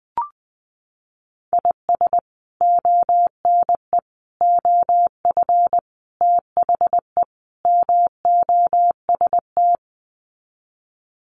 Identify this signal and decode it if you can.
Morse code, heard as a keyed tone, about 20 wpm, 700 Hz. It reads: IS ONE OF THE MOST
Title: Sets of 5 Words 20wpm